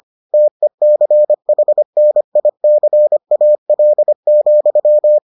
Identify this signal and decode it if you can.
Morse code, heard as a keyed tone, about 25 wpm, 600 Hz. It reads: TECHNICAL,